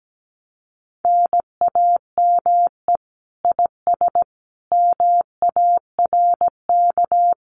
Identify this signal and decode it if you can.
Morse code, heard as a keyed tone, about 17 wpm, 700 Hz. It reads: NAME IS MARK
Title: Common QSO Phrases Encore 17wpm